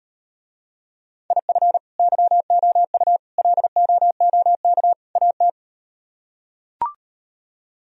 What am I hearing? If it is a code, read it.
Morse code, heard as a keyed tone, about 38 wpm, 700 Hz. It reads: IF YOU LOOK AT